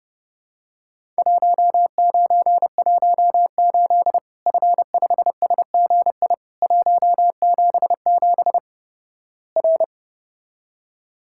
Morse code, heard as a keyed tone, about 32 wpm, 700 Hz. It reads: 1918 F5SGI 177 R
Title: Worked All Europe DX Contest - QTC Exchange - 30wpm